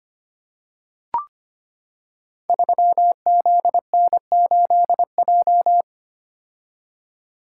Morse code, heard as a keyed tone, about 25 wpm, 700 Hz. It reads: 3ZN8J
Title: Code Groups - 5 Letter:Numbers - 25wpm